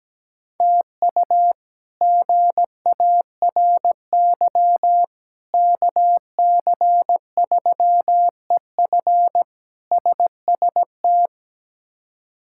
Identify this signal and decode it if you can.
Morse code, heard as a keyed tone, about 17 wpm, 700 Hz. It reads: TU GARY KC3EF SST
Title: K1USN SST QSO Practice - 17wpm